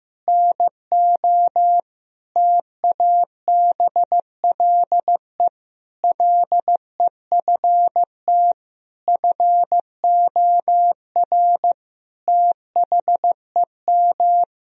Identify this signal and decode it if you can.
Morse code, heard as a keyed tone, about 16 wpm, 700 Hz. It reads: NO TABLE LEFT FOR THEM
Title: Sentences from Top 1000 Words - Review Part 2 - 15wpm